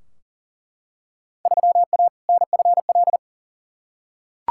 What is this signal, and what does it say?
Morse code, heard as a keyed tone, about 40 wpm, 700 Hz. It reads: 3A NFL
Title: Field Day Exchanges - 40wpm